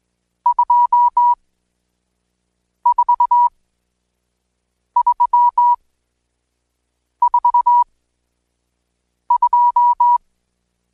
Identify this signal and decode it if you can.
Morse code, heard as a keyed tone, about 20 wpm, spaced as for 5 wpm, 970 Hz. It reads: 24342
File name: Code Quick / AudioDownld1 / Seg76-Day8-DayDrill 8